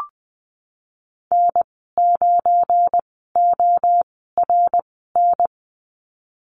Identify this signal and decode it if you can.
Morse code, heard as a keyed tone, about 20 wpm, spaced as for 15 wpm, 700 Hz. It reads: N9ORN
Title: 193 Lesson 31g - Letter v - Call Signs 20wpm